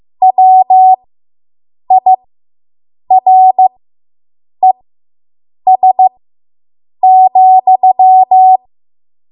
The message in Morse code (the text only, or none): WIRES,